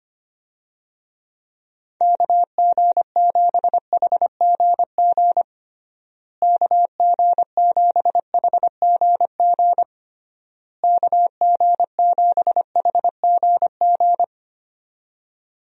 Morse code, heard as a keyed tone, about 25 wpm, 700 Hz. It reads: KG7HGG KG7HGG KG7HGG
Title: USA Call Signs - 6 Characters - Repeated 3x - 25wpm